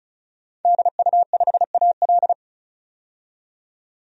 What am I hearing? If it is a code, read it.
Morse code, heard as a keyed tone, about 34 wpm, 700 Hz. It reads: DU5AL
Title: DXCC-100 Call Signs - 5 Characters - 35wpm